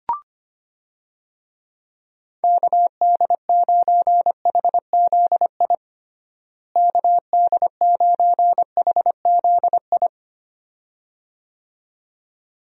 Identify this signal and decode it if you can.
Morse code, heard as a keyed tone, about 25 wpm, 700 Hz. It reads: KD9HZI KD9HZI
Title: USA Call Signs - 6 Characters - Repeated 2x - 25wpm